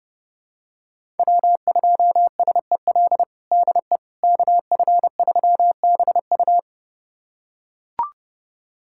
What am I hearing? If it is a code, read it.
Morse code, heard as a keyed tone, about 32 wpm, 700 Hz. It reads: W2SEL DE KF3BU
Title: USA Callsign DE Callsign - 5 Characters - 30wpm